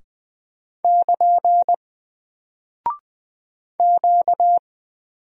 Morse code, heard as a keyed tone, about 20 wpm, 700 Hz. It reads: <KN> Q
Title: Single Character (Letter:Number:Punctuation:Prosign) Alt 20wpm